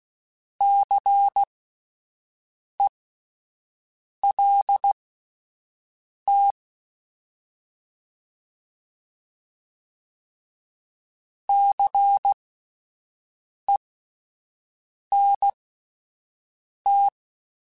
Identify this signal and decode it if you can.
Morse code, heard as a keyed tone, about 16 wpm, spaced as for 5 wpm, 800 Hz. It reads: CELT CENT
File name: Spaced Alphas / Spaced Alphas mp3s / Cs sp @10